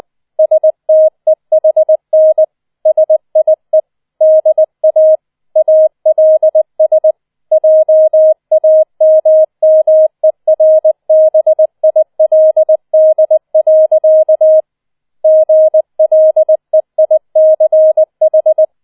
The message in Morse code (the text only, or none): STEHN SIE DA ALS JAMMERBILD. GLEICH